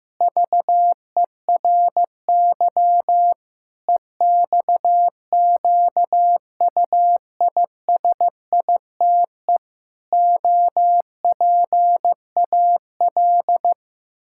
Morse code, heard as a keyed tone, about 15 wpm, 700 Hz. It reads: VERY EXQUISITE OPAL
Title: Pangram Challenge v2 - 15wpm